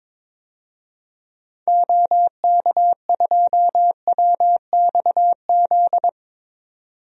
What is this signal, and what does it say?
Morse code, heard as a keyed tone, about 22 wpm, 700 Hz. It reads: OK2WXZ
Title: DXCC-100 Call Signs - 6 Characters - 22wpm